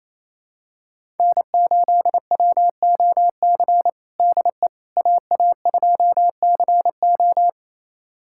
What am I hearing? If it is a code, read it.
Morse code, heard as a keyed tone, about 28 wpm, 700 Hz. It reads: N8WOC DE AA2CO
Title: USA Callsign DE Callsign - 5 Characters - 28wpm